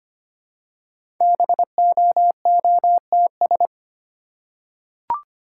Morse code, heard as a keyed tone, about 25 wpm, 700 Hz. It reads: BOOTS
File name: practice / 25wpm / Easy Words with Hints - 25wpm